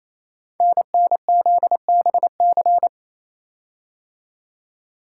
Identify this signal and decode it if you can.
Morse code, heard as a keyed tone, about 27 wpm, 700 Hz. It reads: NNZBC
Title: Code Groups - 5 Letters - 28wpm